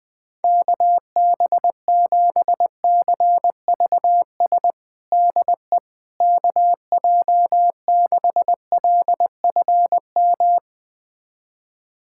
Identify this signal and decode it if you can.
Morse code, heard as a keyed tone, about 20 wpm, 700 Hz. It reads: KB7CVS DE KJ6LFM
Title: USA Callsign DE Callsign - 6 Characters - 20wpm